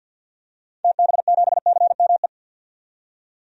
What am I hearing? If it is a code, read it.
Morse code, heard as a keyed tone, about 46 wpm, 700 Hz. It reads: TB7CME
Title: DXCC-100 Call Signs - 6 Characters - 50wpm